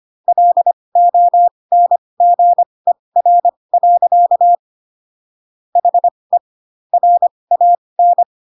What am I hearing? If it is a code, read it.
Morse code, heard as a keyed tone, about 25 wpm, spaced as for 20 wpm, 700 Hz. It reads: LONGER. HE RAN